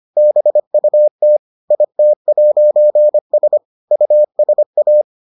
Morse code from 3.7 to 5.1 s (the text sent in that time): USA